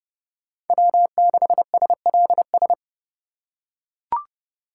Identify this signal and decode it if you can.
Morse code, heard as a keyed tone, about 28 wpm, 700 Hz. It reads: W6SLS